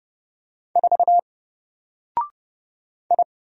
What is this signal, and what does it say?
Morse code, heard as a keyed tone, about 30 wpm, 700 Hz. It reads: 4 I
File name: Single-Letter-Number-30wpm